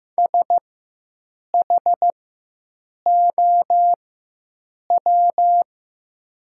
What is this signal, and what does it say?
Morse code, heard as a keyed tone, about 15 wpm, spaced as for 7 wpm, 700 Hz. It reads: SHOW
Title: 4x Character Spacing - Top 200 Words - 15wpm